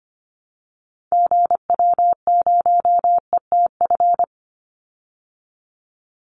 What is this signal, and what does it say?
Morse code, heard as a keyed tone, about 25 wpm, 700 Hz. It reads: GW0ETF
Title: Call Signs Repeated 2x - 25wpm